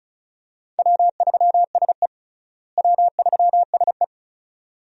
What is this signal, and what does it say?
Morse code, heard as a keyed tone, about 36 wpm, 700 Hz. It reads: W3SE W3SE
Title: USA Call Signs - 4 Characters - Repeated 2x - 35wpm